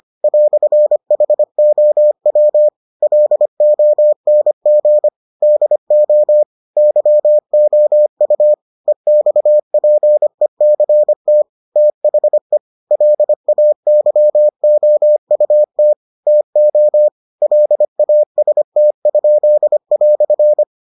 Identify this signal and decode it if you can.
Morse code, heard as a keyed tone, about 25 wpm, 600 Hz. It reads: "HOW LONG DO YOU EXPECT THE LAYOUT TO LAST?"